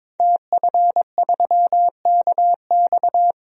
Morse code, heard as a keyed tone, about 22 wpm, 700 Hz. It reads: TF3KX
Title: DXCC-100 Call Signs - 5 Characters - Repeated 2x - 22wpm